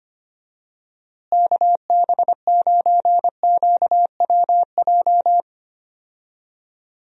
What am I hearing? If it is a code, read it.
Morse code, heard as a keyed tone, about 25 wpm, 700 Hz. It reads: KB9QWJ
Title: USA Call Signs - 6 Characters - Repeated 2x - 25wpm